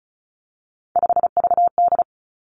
Morse code, heard as a keyed tone, about 36 wpm, 700 Hz. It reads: 5VD
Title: Three Challenging Letter-Number Combinations 35wpm